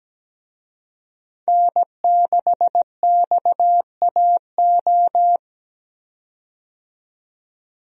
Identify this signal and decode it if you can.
Morse code, heard as a keyed tone, about 17 wpm, 700 Hz. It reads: N6XAO
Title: USA Call Signs - 5 Characters - 17wpm